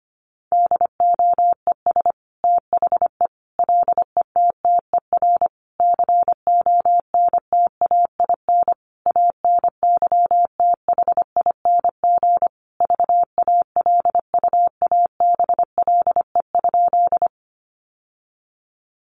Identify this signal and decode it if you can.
Morse code, heard as a keyed tone, about 25 wpm, 700 Hz. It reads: DOES THE LETTER CONTAIN ANYTHING VALUABLE?
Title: Sentences from Top 1100 Words 25wpm